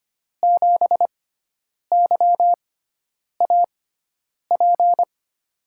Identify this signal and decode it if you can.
Morse code, heard as a keyed tone, about 25 wpm, spaced as for 9 wpm, 700 Hz. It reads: 7YAP
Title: USA Call Signs - 5 Characters - 6x Character Spacing - 25wpm